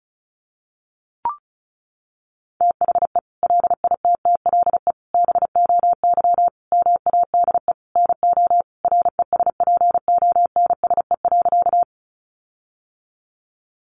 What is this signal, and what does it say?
Morse code, heard as a keyed tone, about 34 wpm, 700 Hz. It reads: THE LITTLE BOY MADE NO RESPONSE.
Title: Sentences from Top 1400 Words 35wpm